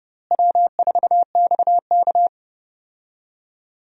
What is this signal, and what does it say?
Morse code, heard as a keyed tone, about 31 wpm, 700 Hz. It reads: W4XK
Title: USA Call Signs - 4 Characters - Repeated 3x - 30wpm